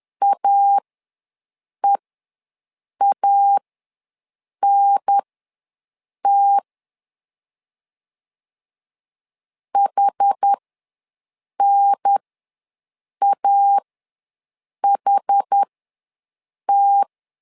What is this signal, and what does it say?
Morse code, heard as a keyed tone, about 10 wpm, spaced as for 6 wpm, 800 Hz. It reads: AEANT HNAHT
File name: 2 UHDN ws 10wpm